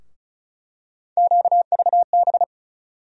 Morse code, heard as a keyed tone, about 36 wpm, 700 Hz. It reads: QVB
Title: USA Call Sign Postfixes - 3 Letters 35wpm